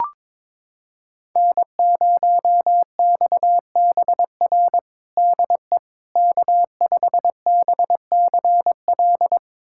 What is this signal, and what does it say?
Morse code, heard as a keyed tone, about 22 wpm, 700 Hz. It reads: N0XBR DE K5BCL
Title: USA Callsign DE Callsign - 5 Characters - 22wpm